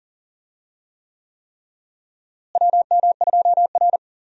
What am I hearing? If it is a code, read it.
Morse code, heard as a keyed tone, about 40 wpm, 700 Hz. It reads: WM2R
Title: USA Call Signs - 4 Characters - Repeated 3x - 40wpm